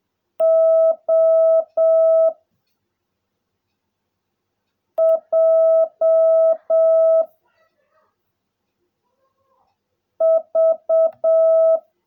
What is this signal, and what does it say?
Morse code, heard as a keyed tone, about 7 wpm, 660 Hz. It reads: O J V
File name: jh8cbh / 2021cw / cw16a